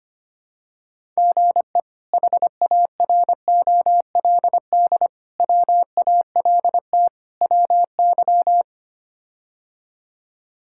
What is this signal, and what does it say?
Morse code, heard as a keyed tone, about 25 wpm, 700 Hz. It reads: GE HAROLD WALT WY
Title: K1USN SST QSO Practice - 25wpm